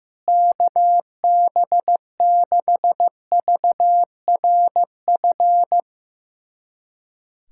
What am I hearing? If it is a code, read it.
Morse code, heard as a keyed tone, about 15 wpm, 700 Hz. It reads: KB6VRF
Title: USA Callsign DE Callsign - 6 Characters - 15wpm